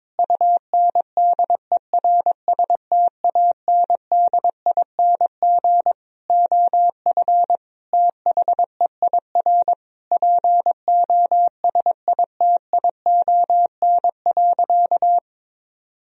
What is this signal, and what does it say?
Morse code, heard as a keyed tone, about 22 wpm, 700 Hz. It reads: UNDERSTANDING OF THEIR POSITION.
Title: Sentences from Top 1000 Words - Review Part 3 - 22wpm